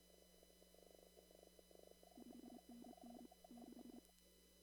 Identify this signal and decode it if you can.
Morse code, heard as a keyed tone, about 28 wpm, 240 Hz. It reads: 5NN 6
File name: CW / kt5j wpx cw-op-k5pi